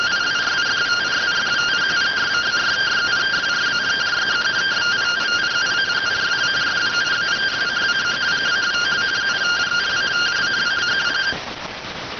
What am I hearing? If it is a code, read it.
Two-tone teleprinter signal, about 170 Hz shift.